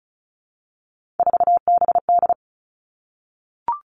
Morse code, heard as a keyed tone, about 36 wpm, 700 Hz. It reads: 4BD
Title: Three Challenging Letter-Number Combinations 35wpm